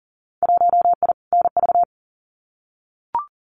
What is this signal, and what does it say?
Morse code, heard as a keyed tone, about 40 wpm, 700 Hz. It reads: 1I NV
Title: Winter Field Day Exchanges Encore 40wpm